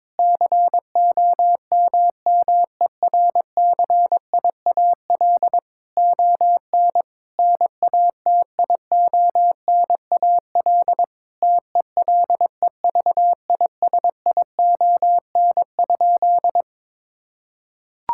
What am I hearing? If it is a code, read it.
Morse code, heard as a keyed tone, about 22 wpm, 700 Hz. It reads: COMMERCIAL ON NATIONAL TELEVISION?
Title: Sentences from Top 1000 Words - Review Part 5 - 22wpm